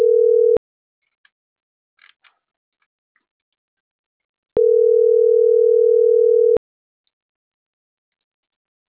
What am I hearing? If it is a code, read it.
Ringback tone (North America), 440 + 480 Hz.